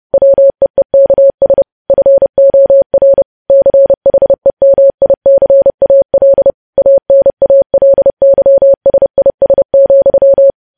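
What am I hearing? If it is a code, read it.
Morse code, heard as a keyed tone, about 30 wpm, 550 Hz. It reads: WEEKS FOR CHEMICAL ANALYSIS,
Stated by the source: Morse code news headlines sent at 30 wpm.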